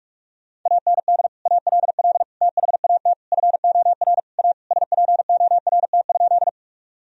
Morse code, heard as a keyed tone, about 46 wpm, 700 Hz. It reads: AND ALL THAT FOR A SPORT?
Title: Sentences from Top 1000 Words - Review Part 4 - 45wpm